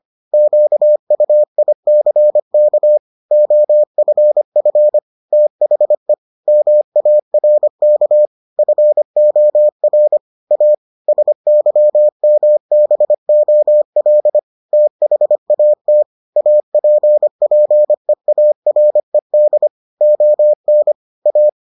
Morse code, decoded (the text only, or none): QUICK OFF THE MARK FOR A SYMBOL THAT APPEARED ON A